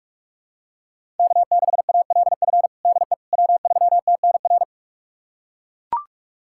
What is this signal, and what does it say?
Morse code, heard as a keyed tone, about 41 wpm, 700 Hz. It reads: K6ALF DE W3TNR
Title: USA Callsign DE Callsign - 5 Characters - 45wpm